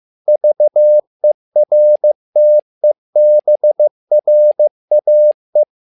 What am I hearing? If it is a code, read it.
Morse code, heard as a keyed tone, about 15 wpm, 600 Hz. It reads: VERTEBRAE
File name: learning center / tower trouble / tower trouble 15wpm